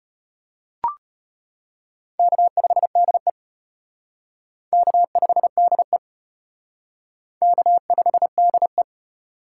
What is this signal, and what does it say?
Morse code, heard as a keyed tone, about 33 wpm, 700 Hz. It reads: K5DE K5DE K5DE